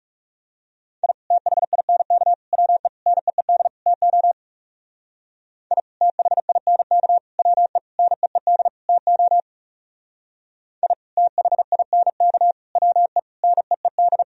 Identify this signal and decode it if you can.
Morse code, heard as a keyed tone, about 37 wpm, 700 Hz. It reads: I THINK WE NEED TO I THINK WE NEED TO I THINK WE NEED
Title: Speed-Racing - Sets of 5 Words 30wpm